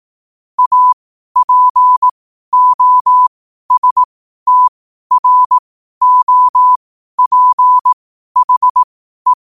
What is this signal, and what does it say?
Morse code, heard as a keyed tone, about 18 wpm, spaced as for 12 wpm, 1000 Hz. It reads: APOSTROPHE
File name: qotdfiles / 12wpm